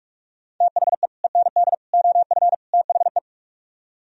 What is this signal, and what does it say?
Morse code, heard as a keyed tone, about 44 wpm, 700 Hz. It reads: THE END OF THE